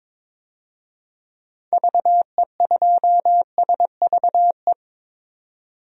Morse code, heard as a keyed tone, about 22 wpm, 700 Hz. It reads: VE2SVE
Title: DXCC-100 Call Signs - 6 Characters - 22wpm